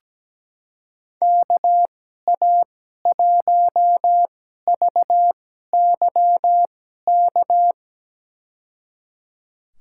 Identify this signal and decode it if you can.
Morse code, heard as a keyed tone, about 17 wpm, spaced as for 12 wpm, 700 Hz. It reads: KA1VYK
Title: USA Call Signs - 2x Character Spacing - 17wpm